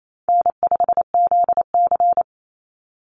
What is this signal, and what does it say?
Morse code, heard as a keyed tone, about 28 wpm, 700 Hz. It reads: N5ZC
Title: Easy Call Signs - 28wpm